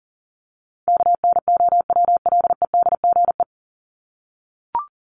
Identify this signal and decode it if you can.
Morse code, heard as a keyed tone, about 40 wpm, 700 Hz. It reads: KNOWLEDGE
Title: Top 501-750 Words 40wpm